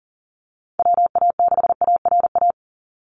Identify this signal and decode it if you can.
Morse code, heard as a keyed tone, about 42 wpm, 700 Hz. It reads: WA6ARA
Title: NA SOTA Chaser Callsigns 40wpm